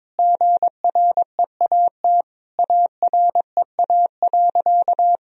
Morse code, heard as a keyed tone, about 22 wpm, 700 Hz. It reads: GREAT AREA.
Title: Sentences from Top 1000 Words - Review Part 2 - 22wpm